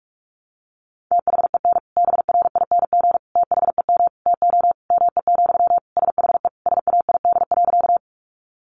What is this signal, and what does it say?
Morse code, heard as a keyed tone, about 44 wpm, 700 Hz. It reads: THEN BRING THEM TO ME, SHE SAID.
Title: Sentences from Top 500 Words 45wpm